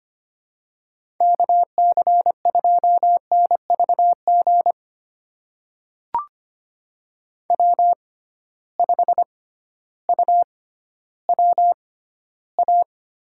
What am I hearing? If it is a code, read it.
Morse code, heard as a keyed tone, about 25 wpm, spaced as for 14 wpm, 700 Hz. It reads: KC2NVG W5UWA
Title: USA Call Signs - 6x Character Spacing - 25wpm